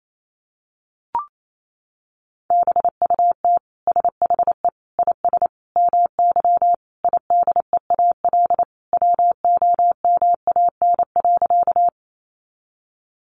Morse code, heard as a keyed tone, about 28 wpm, 700 Hz. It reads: BUT SHE IS MY IDEAL WOMAN.
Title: Sentences from Top 1400 Words 28wpm